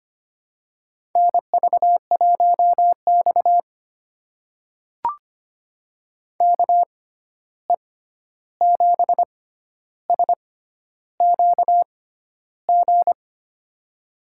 Morse code, heard as a keyed tone, about 25 wpm, spaced as for 11 wpm, 700 Hz. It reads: NV1X KE7SQG
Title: USA Call Signs - 6x Character Spacing - 25wpm